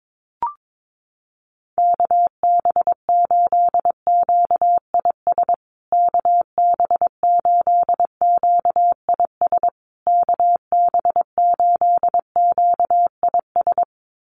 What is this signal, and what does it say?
Morse code, heard as a keyed tone, about 22 wpm, 700 Hz. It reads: KB8QIS KB8QIS KB8QIS
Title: USA Call Signs Repeated 3x 22wpm